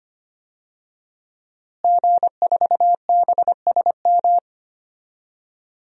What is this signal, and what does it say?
Morse code, heard as a keyed tone, about 25 wpm, 700 Hz. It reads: G4BSM
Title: DXCC-100 Call Signs - 5 Characters - Repeated 3x - 25wpm